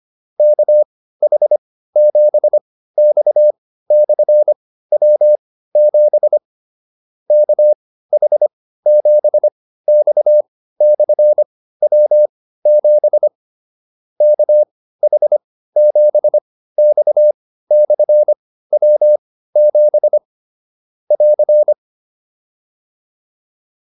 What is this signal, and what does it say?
Morse code, heard as a keyed tone, about 25 wpm, spaced as for 15 wpm, 600 Hz. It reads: KH7X/W7 KH7X/W7 KH7X/W7 <AR>
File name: QSO104 15